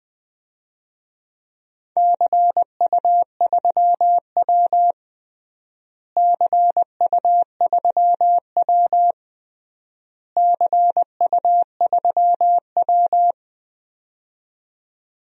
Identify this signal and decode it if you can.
Morse code, heard as a keyed tone, about 20 wpm, 700 Hz. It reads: CU3W CU3W CU3W
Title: DXCC-100 Call Signs - 4 Characters - Repeated 3x - 20wpm